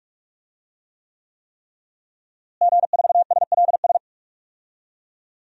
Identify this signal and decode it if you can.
Morse code, heard as a keyed tone, about 46 wpm, 700 Hz. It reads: G4SLS
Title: DXCC-100 Call Signs - 5 Characters - Repeated 3x - 45wpm